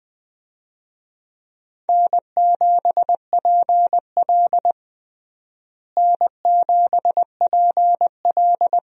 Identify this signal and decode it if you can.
Morse code, heard as a keyed tone, about 20 wpm, 700 Hz. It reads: N7PL N7PL
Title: USA Call Signs - 4 Characters - Repeated 3x - 20wpm